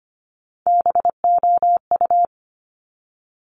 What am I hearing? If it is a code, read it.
Morse code, heard as a keyed tone, about 25 wpm, 700 Hz. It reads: BOU
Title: Top Three Letter Combinations in Words 25wpm